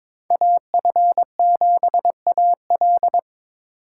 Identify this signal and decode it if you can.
Morse code, heard as a keyed tone, about 26 wpm, 700 Hz. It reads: AF7AL
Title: Speed Racing - USA 2x2 Call Signs 22wpm